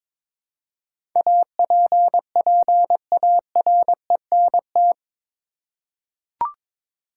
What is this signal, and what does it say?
Morse code, heard as a keyed tone, about 22 wpm, 700 Hz. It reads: APPARENT
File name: practice / 22wpm / Eight Letter Words - 22wpm